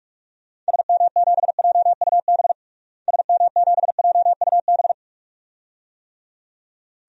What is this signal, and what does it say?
Morse code, heard as a keyed tone, about 46 wpm, 700 Hz. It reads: SM7JUB SM7JUB
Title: DXCC-100 Call Signs - 6 Characters - Repeated 2x - 45wpm